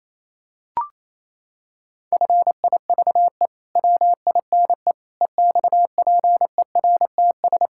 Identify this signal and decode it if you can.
Morse code, heard as a keyed tone, about 28 wpm, 700 Hz. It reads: FIVE WINE EXPERTS
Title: Pangram Challenge v2 - 28wpm